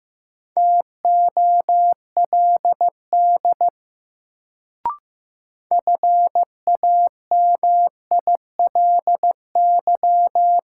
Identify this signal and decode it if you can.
Morse code, heard as a keyed tone, about 15 wpm, 700 Hz. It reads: TOLD FAMILY
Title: Sentences from Top 400 Words - New Word Drill - 15wpm